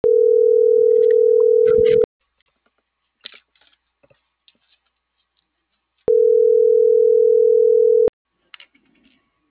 Ringback tone (North America), 440 + 480 Hz.